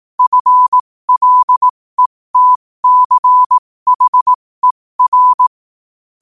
Morse code, heard as a keyed tone, about 18 wpm, spaced as for 15 wpm, 1000 Hz. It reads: FLETCHER
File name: qotdfiles / 15wpm